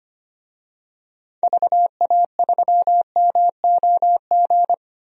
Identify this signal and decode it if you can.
Morse code, heard as a keyed tone, about 25 wpm, 700 Hz. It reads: VA3MOG
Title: DXCC-100 Call Signs - 6 Characters - Repeated 2x - 25wpm